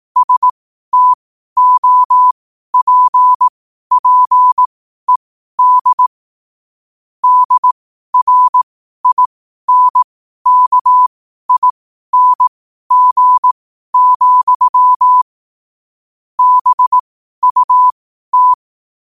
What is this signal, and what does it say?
Morse code, heard as a keyed tone, about 18 wpm, spaced as for 12 wpm, 1000 Hz. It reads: STOPPED DRINKING, BUT
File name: qotdfiles / 12wpm